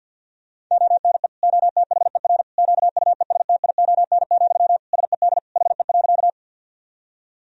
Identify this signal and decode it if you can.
Morse code, heard as a keyed tone, about 51 wpm, 700 Hz. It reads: ONE OTHER QUESTION, SED HE.
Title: Sentences from Top 400 Words and Abbreviations 50wpm